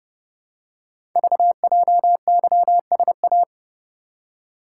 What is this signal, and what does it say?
Morse code, heard as a keyed tone, about 30 wpm, 700 Hz. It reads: VJYSA
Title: Code Groups - 5 Letters - 30wpm